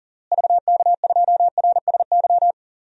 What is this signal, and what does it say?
Morse code, heard as a keyed tone, about 40 wpm, 700 Hz. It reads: VK2RSY
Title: DXCC-100 Call Signs - 6 Characters - Repeated 2x - 40wpm